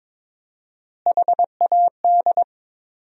Morse code, heard as a keyed tone, about 22 wpm, 700 Hz. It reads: HAD
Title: 4x Word-Spacing - Sentences from Top 300 Words - 22wpm